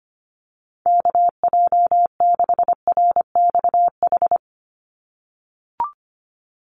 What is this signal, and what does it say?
Morse code, heard as a keyed tone, about 25 wpm, 700 Hz. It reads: KJ6RXH
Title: USA Call Signs 25wpm